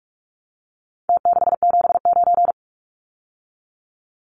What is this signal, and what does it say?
Morse code, heard as a keyed tone, about 44 wpm, 700 Hz. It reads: T679
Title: Numbers - Contest Style - 4-Digit 45wpm